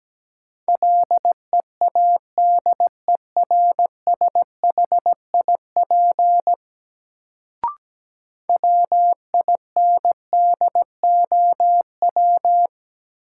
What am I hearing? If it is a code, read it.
Morse code, heard as a keyed tone, about 17 wpm, 700 Hz. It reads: LEADERSHIP WINDOW
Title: Sentences from Top 1600 Words - New Word Drill - 17wpm